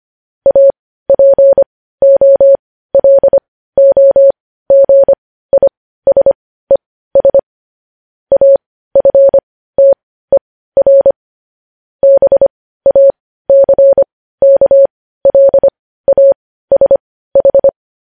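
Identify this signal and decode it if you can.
Morse code, heard as a keyed tone, about 25 wpm, spaced as for 15 wpm, 550 Hz. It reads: APOLOGISES AFTER BACKLASH
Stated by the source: News headlines sent as Morse practice at 15.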